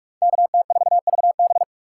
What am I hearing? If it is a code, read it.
Morse code, heard as a keyed tone, about 42 wpm, 700 Hz. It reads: KT4VB
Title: USA Call Signs - 6x Character Spacing - 45wpm